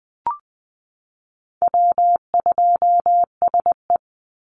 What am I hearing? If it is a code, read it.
Morse code, heard as a keyed tone, about 20 wpm, 700 Hz. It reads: W2SE
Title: NA SOTA Chaser Callsigns 20wpm